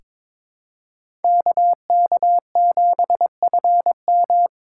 Morse code, heard as a keyed tone, about 22 wpm, 700 Hz. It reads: KK7FM
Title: USA Call Signs - 4x Character Spacing - 22wpm